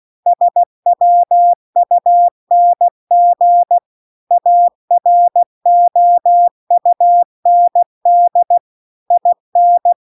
Morse code, heard as a keyed tone, about 16 wpm, 700 Hz. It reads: SWUNG AROUND IN
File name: CW-The-Big-Twoheaded-River-2-16WPM